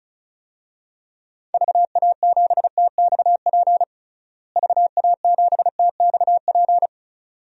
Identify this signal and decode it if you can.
Morse code, heard as a keyed tone, about 36 wpm, 700 Hz. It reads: VA7TXP VA7TXP
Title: DXCC-100 Call Signs - 6 Characters - Repeated 2x - 35wpm